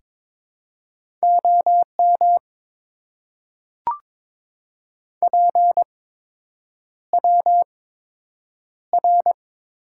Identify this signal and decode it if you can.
Morse code, heard as a keyed tone, about 22 wpm, spaced as for 8 wpm, 700 Hz. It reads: OM PWR
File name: Top 90 QSO Elements - 8x Character Spacing - 22wpm